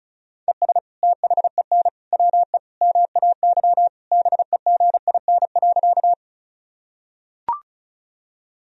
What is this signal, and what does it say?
Morse code, heard as a keyed tone, about 34 wpm, 700 Hz. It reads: ES THEN WE MAY BEGIN.
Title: Sentences from Top 300 Words and Abbreviations 35wpm